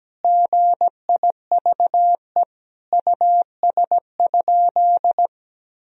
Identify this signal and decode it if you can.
Morse code, heard as a keyed tone, about 17 wpm, 700 Hz. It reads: GIVE US?
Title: Sentences from Top 1000 Words - Review Part 4 - 17wpm